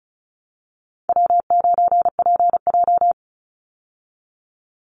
Morse code, heard as a keyed tone, about 34 wpm, 700 Hz. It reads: W9PJ
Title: Easy Call Signs Repeated 3x - 35wpm